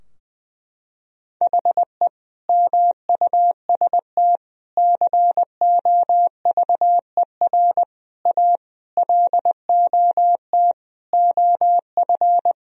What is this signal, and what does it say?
Morse code, heard as a keyed tone, about 20 wpm, 700 Hz. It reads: HE MUST COVER A LOT OF
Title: Sentences from Top 1000 Words - Review Part 3 - 20wpm